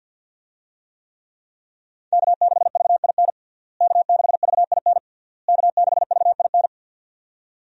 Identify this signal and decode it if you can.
Morse code, heard as a keyed tone, about 51 wpm, 700 Hz. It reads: K6VIN K6VIN K6VIN
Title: USA Call Signs - 5 Characters - Repeated 3x - 50wpm